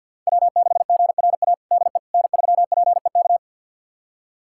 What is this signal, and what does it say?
Morse code, heard as a keyed tone, about 52 wpm, 700 Hz. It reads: W6GRA DE N3PEK
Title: USA Callsign DE Callsign 50wpm